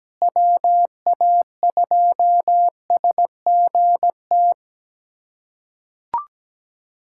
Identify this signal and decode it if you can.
Morse code, heard as a keyed tone, about 17 wpm, 700 Hz. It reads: WA2SGT
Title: USA Callsign DE Callsign 17wpm